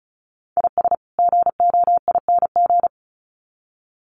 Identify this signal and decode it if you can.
Morse code, heard as a keyed tone, about 34 wpm, 700 Hz. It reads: IS GOING
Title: Sets of 2 Words 35wpm